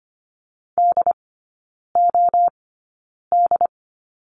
Morse code, heard as a keyed tone, about 25 wpm, spaced as for 9 wpm, 690 Hz. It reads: DOD
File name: Lesson 15e - Number 5 - Three Characters 25wpm